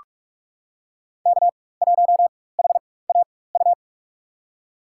Morse code, heard as a keyed tone, about 45 wpm, spaced as for 21 wpm, 700 Hz. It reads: K1HAU
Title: USA Call Signs - 4x Character Spacing - 45wpm